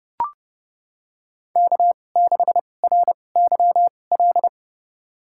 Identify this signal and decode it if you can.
Morse code, heard as a keyed tone, about 32 wpm, spaced as for 22 wpm, 700 Hz. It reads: K6RYL
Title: USA Call Signs - 5 Characters - 2x Character Spacing - 30wpm